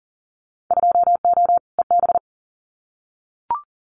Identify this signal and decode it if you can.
Morse code, heard as a keyed tone, about 41 wpm, 700 Hz. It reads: 2O EB
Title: Winter Field Day Exchanges Encore 40wpm